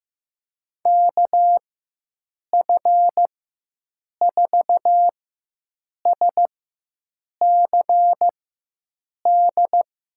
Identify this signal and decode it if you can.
Morse code, heard as a keyed tone, about 15 wpm, spaced as for 7 wpm, 700 Hz. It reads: KF4SCD
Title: USA Call Signs - 4x Character Spacing - 15wpm